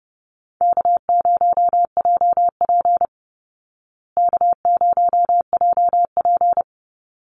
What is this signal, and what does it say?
Morse code, heard as a keyed tone, about 30 wpm, 700 Hz. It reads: K0JP K0JP
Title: Call Signs Repeated 3x - 30wpm